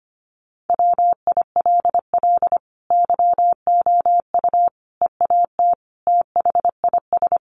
Morse code, heard as a keyed tone, about 25 wpm, 700 Hz. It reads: WILL YOU EAT THIS
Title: Sentences from Top 1300 Words 25wpm